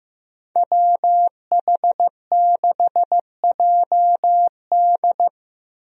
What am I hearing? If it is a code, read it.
Morse code, heard as a keyed tone, about 15 wpm, 700 Hz. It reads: WH6JD
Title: USA Call Signs - 5 Characters - Repeated 3x - 15wpm